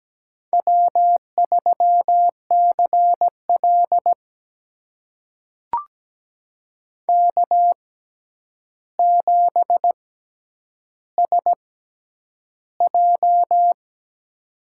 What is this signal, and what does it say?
Morse code, heard as a keyed tone, about 17 wpm, spaced as for 9 wpm, 700 Hz. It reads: W3CL K7SJ
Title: USA Call Signs - 4 Characters - 6x Character Spacing - 17wpm